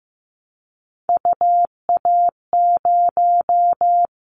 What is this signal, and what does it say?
Morse code, heard as a keyed tone, about 15 wpm, 700 Hz. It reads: UA0
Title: Callsign Prefixes for Top 100 DXCC - 15wpm